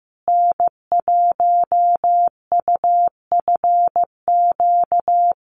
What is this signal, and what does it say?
Morse code, heard as a keyed tone, about 15 wpm, 700 Hz. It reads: N1UFQ
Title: USA Call Signs Repeated 3x 15wpm